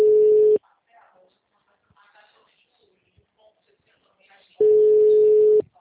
Ringback tone (Europe, 425 Hz).